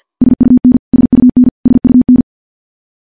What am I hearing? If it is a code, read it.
Morse code, heard as a keyed tone, about 50 wpm, 250 Hz. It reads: SUN SUN SUN
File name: morse / morse-training-50WPM